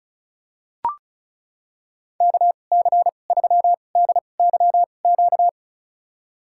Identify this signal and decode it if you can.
Morse code, heard as a keyed tone, about 36 wpm, spaced as for 25 wpm, 700 Hz. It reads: KC3DYQ
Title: USA Call Signs - 6 Characters - 2x Character Spacing - 35wpm